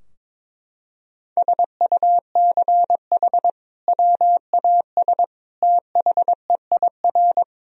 Morse code, heard as a keyed tone, about 22 wpm, 700 Hz. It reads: SUCH WAS THEIR
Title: Sentences from Top 1000 Words - Review Part 2 - 22wpm